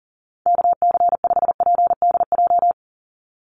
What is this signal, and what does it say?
Morse code, heard as a keyed tone, about 40 wpm, 700 Hz. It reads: KC5PDJ
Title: USA Call Signs 40wpm